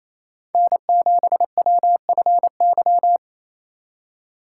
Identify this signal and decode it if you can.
Morse code, heard as a keyed tone, about 36 wpm, 700 Hz. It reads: N7WFY